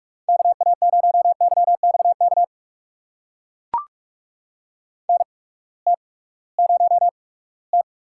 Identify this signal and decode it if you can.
Morse code, heard as a keyed tone, about 44 wpm, spaced as for 22 wpm, 700 Hz. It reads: KA0YXK NT0T
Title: USA Call Signs - 8x Character Spacing - 45wpm